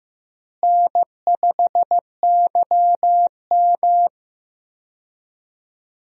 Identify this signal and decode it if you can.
Morse code, heard as a keyed tone, about 15 wpm, 700 Hz. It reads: N5YM
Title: USA Call Signs - 4 Characters - Repeated 3x - 15wpm